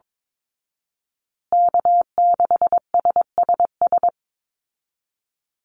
Morse code, heard as a keyed tone, about 22 wpm, 700 Hz. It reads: K6SSS
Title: Call Signs Repeated 3x - 22wpm